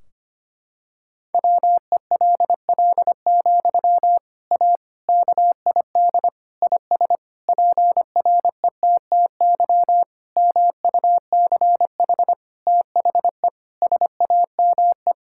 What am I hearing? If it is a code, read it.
Morse code, heard as a keyed tone, about 25 wpm, 700 Hz. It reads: WELL, A KID IS PRETTY MUCH THE SAME
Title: Sentences from Top 1600 Words 25wpm